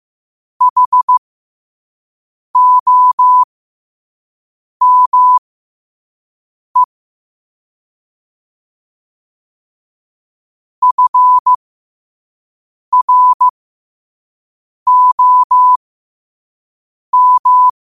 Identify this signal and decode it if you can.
Morse code, heard as a keyed tone, about 15 wpm, spaced as for 5 wpm, 1000 Hz. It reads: HOME FROM
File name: qotdfiles / 5wpm